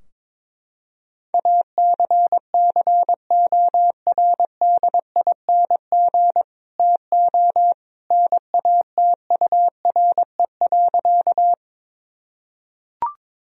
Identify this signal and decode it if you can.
Morse code, heard as a keyed tone, about 22 wpm, 700 Hz. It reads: ACCORDING TO NATURE.
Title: Sentences from Top 1000 Words - Review Part 4 - 22wpm